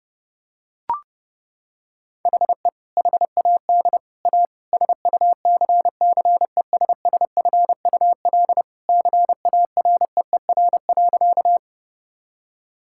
Morse code, heard as a keyed tone, about 30 wpm, 700 Hz. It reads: HE HAD A SUCCESSFUL CAREER.
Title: Sentences from Top 1000 Words - Review Part 6 - 30wpm